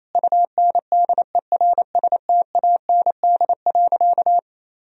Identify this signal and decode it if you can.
Morse code, heard as a keyed tone, about 28 wpm, 700 Hz. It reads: UNDERSTAND.
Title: Sentences from Top 1000 Words - Review Part 3 - 28wpm